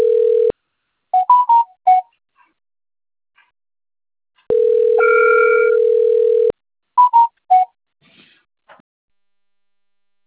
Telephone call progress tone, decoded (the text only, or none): ringback tone